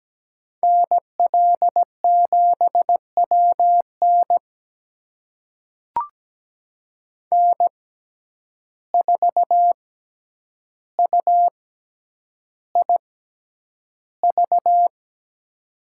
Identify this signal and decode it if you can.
Morse code, heard as a keyed tone, about 17 wpm, spaced as for 9 wpm, 700 Hz. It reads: NL7WN N4UIV
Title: USA Call Signs - 5 Characters - 6x Character Spacing - 17wpm